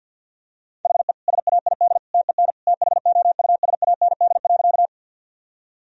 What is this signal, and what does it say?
Morse code, heard as a keyed tone, about 52 wpm, 700 Hz. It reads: HE SAID TEN THOUSAND.
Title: Sentences from Top 1000 Words - Review Part 3 - 50wpm